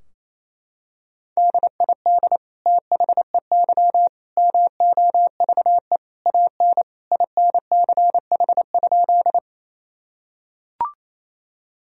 Morse code, heard as a keyed tone, about 28 wpm, 700 Hz. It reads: DID THEY MOVE AN INCH?
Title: Sentences from Top 1000 Words - Review Part 3 - 28wpm